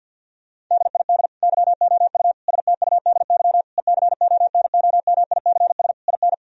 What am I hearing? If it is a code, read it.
Morse code, heard as a keyed tone, about 50 wpm, 700 Hz. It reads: DID YOU STUDY ECONOMICS IN
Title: Sentences from Top 1000 Words - Review Part 3 - 50wpm